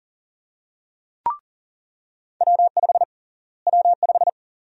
Morse code, heard as a keyed tone, about 40 wpm, 700 Hz. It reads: W5 W5
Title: USA Call Sign Prefixes - Repeated 2x - 40wpm